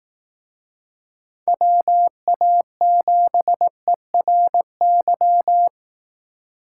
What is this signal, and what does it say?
Morse code, heard as a keyed tone, about 18 wpm, 700 Hz. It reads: WA7ERY
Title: Speed Racing - USA 2x3 Call Signs 15wpm